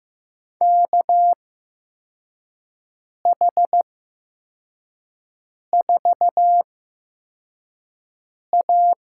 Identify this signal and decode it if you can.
Morse code, heard as a keyed tone, about 15 wpm, spaced as for 4 wpm, 700 Hz. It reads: KH4A
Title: USA Call Signs - 5 Characters - 8x Character Spacing - 15wpm